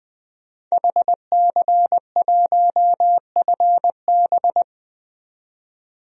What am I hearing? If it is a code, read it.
Morse code, heard as a keyed tone, about 20 wpm, 700 Hz. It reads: HC1FB
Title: DXCC-100 Call Signs - 5 Characters - 20wpm